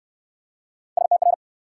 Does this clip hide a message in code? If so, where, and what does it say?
Yes: Morse code from 1.0 to 1.3 s, heard as a keyed tone, about 65 wpm, 700 Hz. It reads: STR